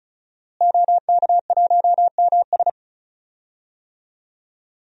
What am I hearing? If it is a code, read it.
Morse code, heard as a keyed tone, about 36 wpm, 700 Hz. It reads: OK1MS
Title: DXCC-100 Call Signs - 5 Characters - Repeated 2x - 35wpm